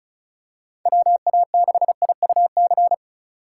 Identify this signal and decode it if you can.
Morse code, heard as a keyed tone, about 34 wpm, 700 Hz. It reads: WA6IUC
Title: USA Call Signs - 6 Characters - 2x Character Spacing - 35wpm